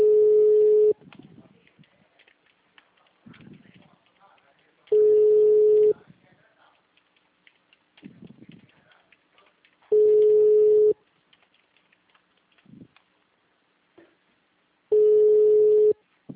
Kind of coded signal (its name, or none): telephone call progress tone